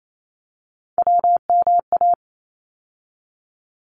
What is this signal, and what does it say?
Morse code, heard as a keyed tone, about 28 wpm, 700 Hz. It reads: WMA
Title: ARRL Sections 28wpm